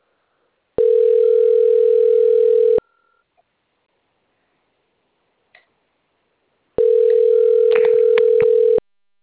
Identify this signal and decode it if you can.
Ringback tone (North America), 440 + 480 Hz.
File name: rg-1600-unknown- - - .61125